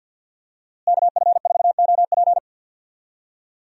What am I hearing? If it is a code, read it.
Morse code, heard as a keyed tone, about 53 wpm, 700 Hz. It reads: KF4OP
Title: USA Call Signs - 5 Characters - Repeated 3x - 50wpm